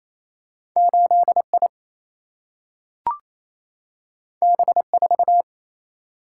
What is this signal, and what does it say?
Morse code, heard as a keyed tone, about 28 wpm, 700 Hz. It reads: 8 I B 4
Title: Single-Letter Single-Number 28wpm